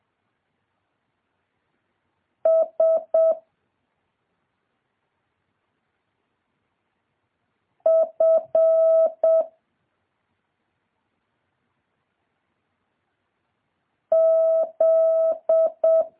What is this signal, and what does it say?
Morse code, heard as a keyed tone, about 7 wpm, 650 Hz. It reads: S F Z
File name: jh8cbh / 2023cw